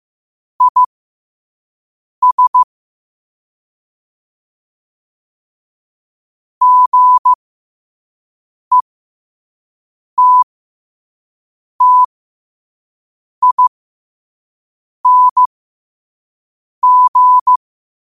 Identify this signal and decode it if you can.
Morse code, heard as a keyed tone, about 15 wpm, spaced as for 5 wpm, 1000 Hz. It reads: IS GETTING
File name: qotdfiles / 5wpm